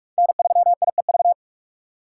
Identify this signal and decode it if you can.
Morse code, heard as a keyed tone, about 44 wpm, 700 Hz. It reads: N3IEV
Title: USA Call Signs - 5 Characters - 6x Character Spacing - 45wpm